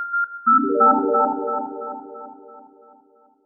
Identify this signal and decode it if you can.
There is no code.